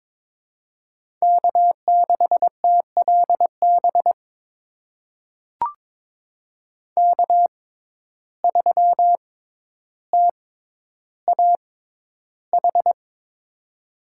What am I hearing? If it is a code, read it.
Morse code, heard as a keyed tone, about 22 wpm, spaced as for 11 wpm, 700 Hz. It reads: K6TLB K3TAH
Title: USA Call Signs - 5 Characters - 6x Character Spacing - 22wpm